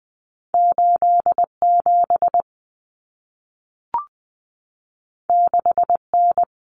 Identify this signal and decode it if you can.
Morse code, heard as a keyed tone, about 20 wpm, 700 Hz. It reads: 87 6N
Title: Common Cut Numbers - 2-Digit 20wpm